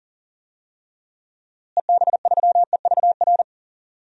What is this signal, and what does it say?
Morse code, heard as a keyed tone, about 40 wpm, 700 Hz. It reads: EB3EVR
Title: DXCC-100 Call Signs - 6 Characters - Repeated 3x - 40wpm